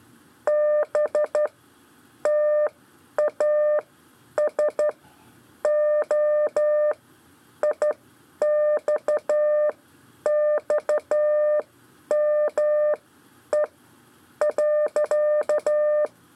Morse code, heard as a keyed tone, about 12 wpm, spaced as for 8 wpm, 600 Hz. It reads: BTASOIXXME.
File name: 4 LEZIONE CW 1